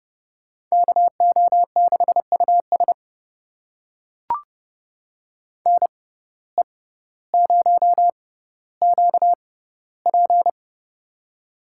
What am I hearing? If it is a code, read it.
Morse code, heard as a keyed tone, about 30 wpm, spaced as for 15 wpm, 700 Hz. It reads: KO6US NE0QP
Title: USA Call Signs - 6x Character Spacing - 30wpm